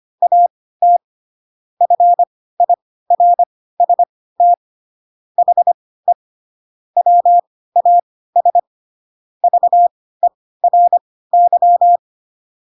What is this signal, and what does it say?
Morse code, heard as a keyed tone, about 25 wpm, spaced as for 16 wpm, 700 Hz. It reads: AT FIRST HE WAS VERY
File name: Pinocchio / Pinocchio ch7 10 25 ik1hge